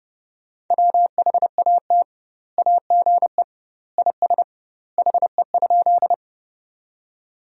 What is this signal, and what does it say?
Morse code, heard as a keyed tone, about 30 wpm, 700 Hz. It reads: WHAT AGE IS HE?
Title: 2x Word-Spacing - Sentences from Top 500 Words - 30wpm